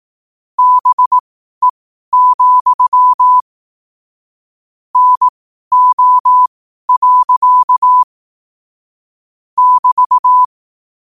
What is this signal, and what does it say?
Morse code, heard as a keyed tone, about 18 wpm, spaced as for 12 wpm, 1000 Hz. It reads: BE, NO. <BT>
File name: qotdfiles / 12wpm